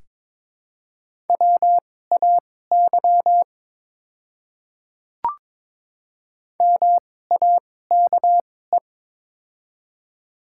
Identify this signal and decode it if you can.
Morse code, heard as a keyed tone, about 22 wpm, spaced as for 16 wpm, 700 Hz. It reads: WAY MAKE
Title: 2x Character Spacing - Top 200 Words - 22wpm